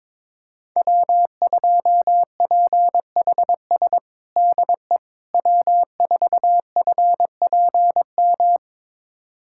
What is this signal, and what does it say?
Morse code, heard as a keyed tone, about 21 wpm, 700 Hz. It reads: W2PHS DE W4FPM
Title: USA Callsign DE Callsign - 5 Characters - 22wpm